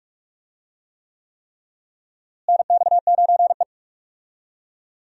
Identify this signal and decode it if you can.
Morse code, heard as a keyed tone, about 41 wpm, 700 Hz. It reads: NX9E
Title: USA Call Signs - 4 Characters - Repeated 3x - 45wpm